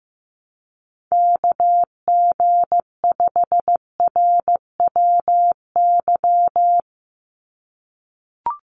Morse code, heard as a keyed tone, about 16 wpm, 700 Hz. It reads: KG5RWY
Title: USA Call Signs 15wpm